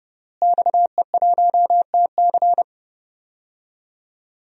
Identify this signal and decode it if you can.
Morse code, heard as a keyed tone, about 29 wpm, 700 Hz. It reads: XE1TC
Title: DXCC-100 Call Signs - 5 Characters - Repeated 3x - 30wpm